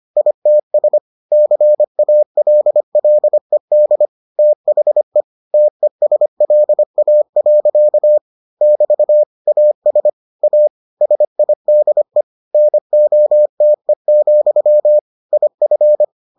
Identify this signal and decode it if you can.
Morse code, heard as a keyed tone, about 25 wpm, 600 Hz. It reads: ITS CALLED THE TESLA. <BT> AS A SIDE NOTE, IF